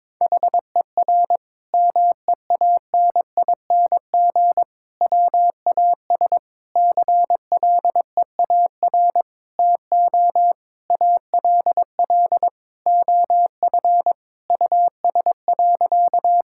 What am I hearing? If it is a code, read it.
Morse code, heard as a keyed tone, about 22 wpm, 700 Hz. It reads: HER MEANING WAS CLEAR TO ALL OF US.
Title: Sentences from Top 1000 Words - Review Part 6 - 22wpm